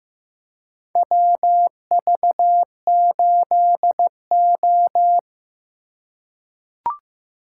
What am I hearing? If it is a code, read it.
Morse code, heard as a keyed tone, about 15 wpm, 700 Hz. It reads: WV8O
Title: USA Call Signs - 4 Characters - 4x Character Spacing - 15wpm